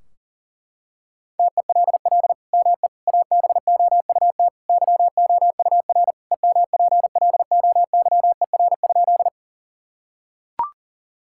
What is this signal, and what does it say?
Morse code, heard as a keyed tone, about 40 wpm, 700 Hz. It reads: TELL ME ABOUT YOUR EMPLOYER?
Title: Sentences from Top 1600 Words 40wpm